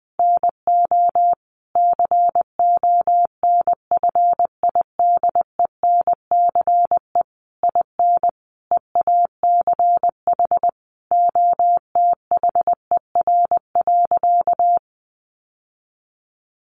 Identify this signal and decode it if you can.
Morse code, heard as a keyed tone, about 19 wpm, 700 Hz. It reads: NO CONFIDENCE IN EACH OTHER.
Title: Sentences from Top 1500 Words 20wpm